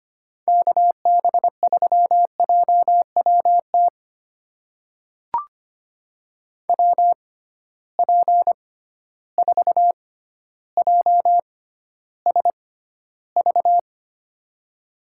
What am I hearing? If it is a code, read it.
Morse code, heard as a keyed tone, about 25 wpm, spaced as for 13 wpm, 700 Hz. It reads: KB3JWT WP4JSV
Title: USA Call Signs - 6 Characters - 6x Character Spacing - 25wpm